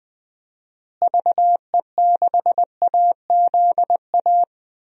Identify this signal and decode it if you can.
Morse code, heard as a keyed tone, about 20 wpm, 700 Hz. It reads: VE6AZA
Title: DXCC-100 Call Signs - 6 Characters - Repeated 3x - 20wpm